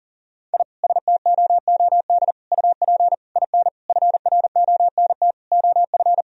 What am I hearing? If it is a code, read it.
Morse code, heard as a keyed tone, about 40 wpm, 700 Hz. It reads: I STOOD UP IN FRONT OF
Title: Sentences from Top 1000 Words - Review Part 2 - 40wpm